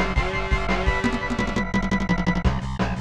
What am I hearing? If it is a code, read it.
There is no code.